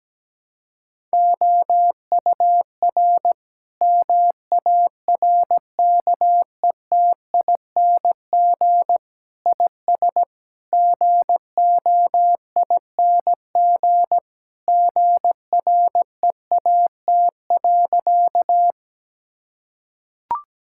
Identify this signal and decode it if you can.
Morse code, heard as a keyed tone, about 17 wpm, 700 Hz. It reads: OUR MARKETING IS GOING GREAT.
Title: Sentences from Top 1000 Words - Review Part 5 - 17wpm